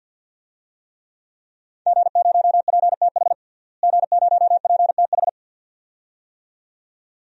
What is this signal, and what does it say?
Morse code, heard as a keyed tone, about 51 wpm, 700 Hz. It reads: G0PTH G0PTH
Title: DXCC-100 Call Signs - 5 Characters - Repeated 2x - 50wpm